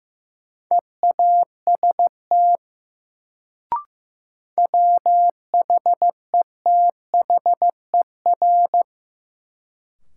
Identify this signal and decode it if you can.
Morse code, heard as a keyed tone, about 15 wpm, 700 Hz. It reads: EAST WHETHER
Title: Sentences from Top 1000 Words - Review Part 3 - New Word Drill - 15wpm